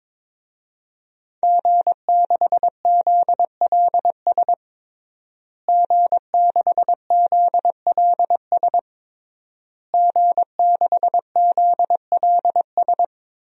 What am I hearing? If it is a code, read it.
Morse code, heard as a keyed tone, about 22 wpm, 700 Hz. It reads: G6ZLS G6ZLS G6ZLS
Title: DXCC-100 Call Signs - 5 Characters - Repeated 3x - 22wpm